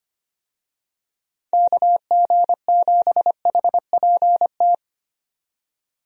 Morse code, heard as a keyed tone, about 25 wpm, 700 Hz. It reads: KG7HPT
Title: USA Call Signs - 6 Characters - 25wpm